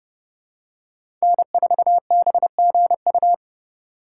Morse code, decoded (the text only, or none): N4BGU